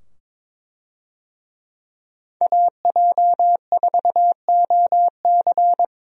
Morse code, heard as a keyed tone, about 22 wpm, 700 Hz. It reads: AJ4OC
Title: USA Call Signs - 5 Characters - 22wpm